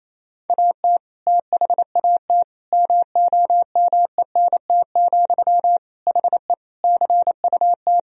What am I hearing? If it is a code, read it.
Morse code, heard as a keyed tone, about 28 wpm, 700 Hz. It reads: AT THAT MOMENT, HE CUT
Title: Sentences from Top 1000 Words - Review Part 1 - 28wpm